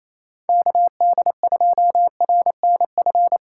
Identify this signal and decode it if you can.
Morse code, heard as a keyed tone, about 28 wpm, 700 Hz. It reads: KD2RNF
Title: USA Call Signs - 6 Characters - 2x Character Spacing - 28wpm